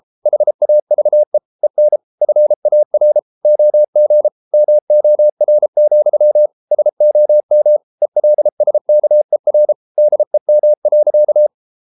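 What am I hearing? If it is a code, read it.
Morse code, heard as a keyed tone, about 33 wpm, 600 Hz. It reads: HAVE EN FAR OG MOR, SOM ELSKER DEM.
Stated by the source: Morse code at 33 wpm, QRQ.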